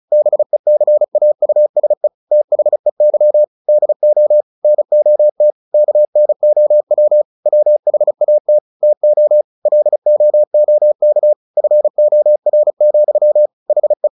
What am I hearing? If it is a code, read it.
Morse code, heard as a keyed tone, about 35 wpm, 600 Hz. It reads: BECAUSE THEY DO NOT KNOW WHAT TO LOOK FOR, HE
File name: the old man and the sea / the old man and the sea 1 35wpm